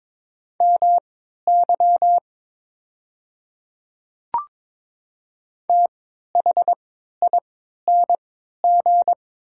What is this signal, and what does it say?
Morse code, heard as a keyed tone, about 22 wpm, spaced as for 13 wpm, 700 Hz. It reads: MY THING
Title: 3x Character Spacing - Top 100 Words - 22wpm